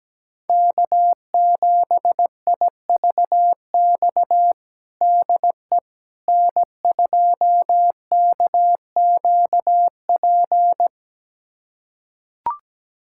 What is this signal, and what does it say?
Morse code, heard as a keyed tone, about 17 wpm, 700 Hz. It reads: K7IVX DE N2KQP
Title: USA Callsign DE Callsign - 5 Characters - 17wpm